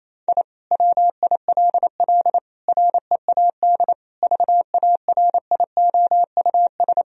Morse code, heard as a keyed tone, about 28 wpm, 700 Hz. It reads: I WILL READ VARIOUS
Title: Sentences from Top 1000 Words - Review Part 4 - 28wpm